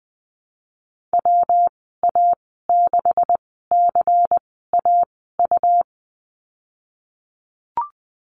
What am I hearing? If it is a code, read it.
Morse code, heard as a keyed tone, about 20 wpm, spaced as for 15 wpm, 700 Hz. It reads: WA6CAU
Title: 147 Lesson 24g - Letter y - Call Signs 20wpm